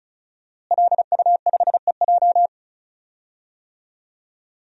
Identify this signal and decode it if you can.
Morse code, heard as a keyed tone, about 36 wpm, 700 Hz. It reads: LU5EJ
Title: DXCC-100 Call Signs - 5 Characters - Repeated 2x - 35wpm